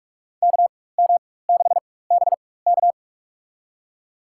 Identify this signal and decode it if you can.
Morse code, heard as a keyed tone, about 46 wpm, spaced as for 21 wpm, 700 Hz. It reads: KM6BK
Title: USA Call Signs - 5 Characters - 4x Character Spacing - 45wpm